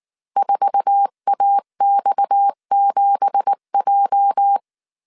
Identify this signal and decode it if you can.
Morse code, heard as a keyed tone, about 19 wpm, 800 Hz. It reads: 4AX7J
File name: CW / DET / 1 9 1500C S B 18wpm 2551